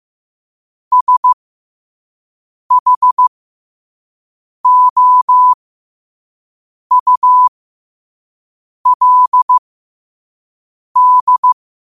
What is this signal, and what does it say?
Morse code, heard as a keyed tone, about 15 wpm, spaced as for 5 wpm, 1000 Hz. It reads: SHOULD
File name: qotdfiles / 5wpm